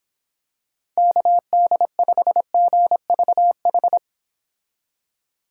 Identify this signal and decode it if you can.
Morse code, heard as a keyed tone, about 26 wpm, 700 Hz. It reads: KD5GVH
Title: Speed Racing - USA 2x3 Call Signs 22wpm